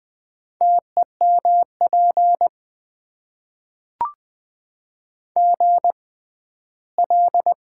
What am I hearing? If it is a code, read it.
Morse code, heard as a keyed tone, about 20 wpm, spaced as for 14 wpm, 700 Hz. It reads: TEMP GL
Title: Top 90 QSO Elements - 6x Character Spacing - 20wpm